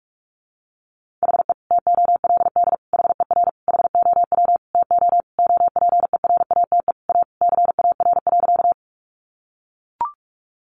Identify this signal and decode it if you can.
Morse code, heard as a keyed tone, about 44 wpm, 700 Hz. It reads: HE TOLD HER HOW TO OPERATE A CAR.
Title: Sentences from Top 1500 Words 45wpm